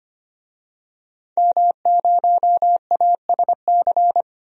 Morse code, heard as a keyed tone, about 25 wpm, 700 Hz. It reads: M0ASC
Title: DXCC-100 Call Signs - 5 Characters - Repeated 2x - 25wpm